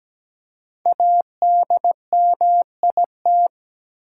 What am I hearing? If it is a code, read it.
Morse code, heard as a keyed tone, about 17 wpm, 700 Hz. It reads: ADMIT
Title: Five Letter Words - 17wpm